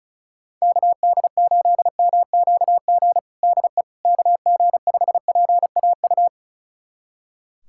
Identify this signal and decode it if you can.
Morse code, heard as a keyed tone, about 36 wpm, 700 Hz. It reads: KD8MQG DE KG5PAU
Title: USA Callsign DE Callsign 35wpm